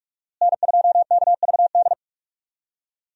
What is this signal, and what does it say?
Morse code, heard as a keyed tone, about 42 wpm, 700 Hz. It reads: N2KVD
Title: USA Call Signs - 5 Characters - Repeated 2x - 45wpm